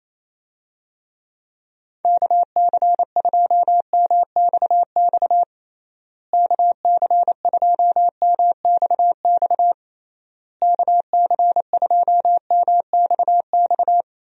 Morse code, heard as a keyed tone, about 27 wpm, 700 Hz. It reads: KC2MXX KC2MXX KC2MXX
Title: USA Call Signs - 6 Characters - Repeated 3x - 28wpm